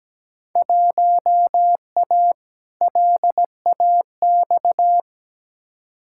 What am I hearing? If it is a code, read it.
Morse code, heard as a keyed tone, about 17 wpm, 700 Hz. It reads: 1A LAX
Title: Field Day Exchanges - 17wpm